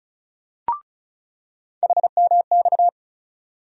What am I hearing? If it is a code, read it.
Morse code, heard as a keyed tone, about 36 wpm, 700 Hz. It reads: HMX
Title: USA Call Sign Postfixes - 3 Letters 35wpm